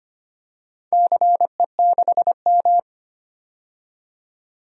Morse code, heard as a keyed tone, about 25 wpm, 700 Hz. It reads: CE6M
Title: DXCC-100 Call Signs - 4 Characters - 25wpm